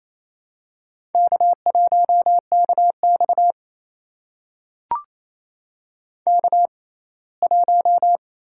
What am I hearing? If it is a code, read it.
Morse code, heard as a keyed tone, about 28 wpm, spaced as for 19 wpm, 700 Hz. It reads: K1KX K1
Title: USA Call Signs - 6x Character Spacing - 28wpm